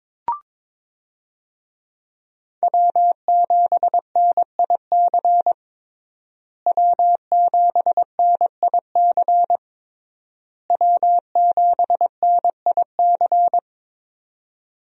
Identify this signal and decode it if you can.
Morse code, heard as a keyed tone, about 22 wpm, 700 Hz. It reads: W7NIC W7NIC W7NIC
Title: USA Call Signs - 5 Characters - Repeated 3x - 22wpm